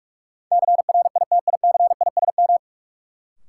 Morse code, heard as a keyed tone, about 46 wpm, 700 Hz. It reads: CRITICISM
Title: Sentences from Top 1300 Words - New Word Drill - 45wpm